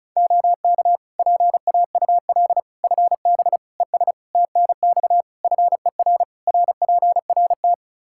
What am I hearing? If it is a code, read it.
Morse code, heard as a keyed tone, about 36 wpm, 700 Hz. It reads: OK PAUL FB ES TNX FER RPRT
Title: K1USN SST QSO Practice 35wpm